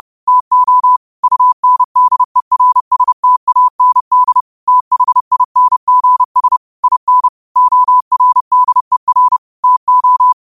Morse code, heard as a keyed tone, about 30 wpm, 1000 Hz. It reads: TO UNDERSTAND THINGS IN ORDER TO
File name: qotdfiles / 30wpm